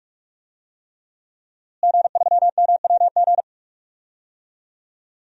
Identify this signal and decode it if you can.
Morse code, heard as a keyed tone, about 46 wpm, 700 Hz. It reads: G3MWG
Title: DXCC-100 Call Signs - 5 Characters - Repeated 3x - 45wpm